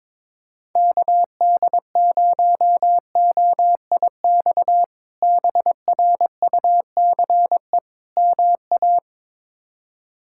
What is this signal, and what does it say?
Morse code, heard as a keyed tone, about 22 wpm, 700 Hz. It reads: KD0OIX BRUCE MA
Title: K1USN SST QSO Practice - 22wpm